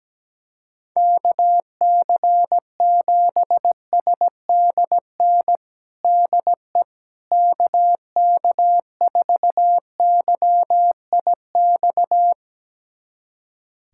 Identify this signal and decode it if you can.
Morse code, heard as a keyed tone, about 17 wpm, 700 Hz. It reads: KC7SDN DE KK4YIX
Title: USA Callsign DE Callsign - 6 Characters - 17wpm